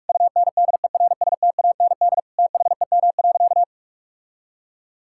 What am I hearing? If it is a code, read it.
Morse code, heard as a keyed tone, about 46 wpm, 700 Hz. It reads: UNDERSTAND THEM.
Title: Sentences from Top 1000 Words - Review Part 5 - 45wpm